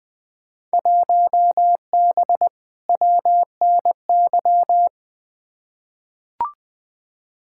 Morse code, heard as a keyed tone, about 20 wpm, 700 Hz. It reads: 1B WNY
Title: Field Day Exchanges - 20wpm